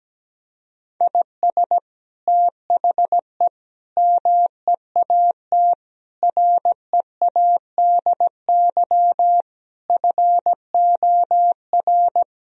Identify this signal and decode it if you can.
Morse code, heard as a keyed tone, about 17 wpm, 700 Hz. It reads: IS THE MEAT READY FOR
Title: Sentences from Top 1000 Words - Review Part 3 - 17wpm